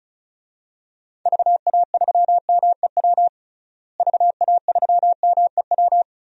Morse code, heard as a keyed tone, about 36 wpm, 700 Hz. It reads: VA3MEW VA3MEW
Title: DXCC-100 Call Signs - 6 Characters - Repeated 3x - 35wpm